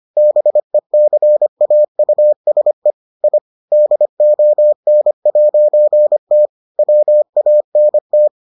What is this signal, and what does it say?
Morse code, heard as a keyed tone, about 25 wpm, 600 Hz. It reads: BECAUSE I DON'T WANT